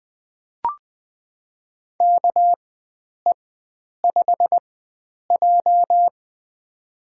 Morse code, heard as a keyed tone, about 20 wpm, spaced as for 9 wpm, 700 Hz. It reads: KE5J
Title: USA Call Signs - 4x Character Spacing - 20wpm